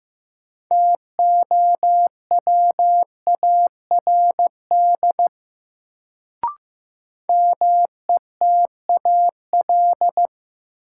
Morse code, heard as a keyed tone, about 15 wpm, 700 Hz. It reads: TOWARD METAL
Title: Sentences from Top 1000 Words - Review Part 6 - New Word Drill - 15wpm